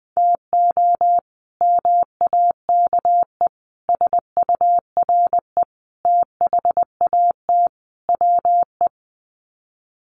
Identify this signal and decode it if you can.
Morse code, heard as a keyed tone, about 20 wpm, 700 Hz. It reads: TO MAKE SURE THAT WE
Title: Sets of 5 Words Encore 20wpm